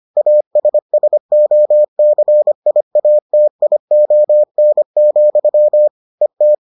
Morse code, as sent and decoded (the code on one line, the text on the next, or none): .- ... ... --- -.-. .. .- - .. --- -. --..-- / . -
ASSOCIATION, ET